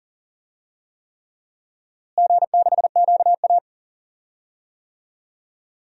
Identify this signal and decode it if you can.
Morse code, heard as a keyed tone, about 40 wpm, 700 Hz. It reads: G6QA
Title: DXCC-100 Call Signs - 4 Characters - Repeated 3x - 40wpm